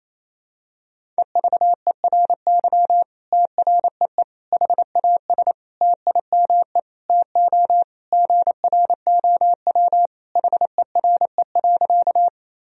Morse code, heard as a keyed tone, about 28 wpm, 700 Hz. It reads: EVERY TREE HAS TIME TO GROW HERE.
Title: Sentences from Top 1000 Words - Review Part 1 - 28wpm